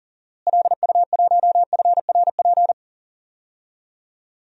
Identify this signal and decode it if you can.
Morse code, heard as a keyed tone, about 38 wpm, 700 Hz. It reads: LU1FRP
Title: DXCC-100 Call Signs - 6 Characters - Repeated 2x - 40wpm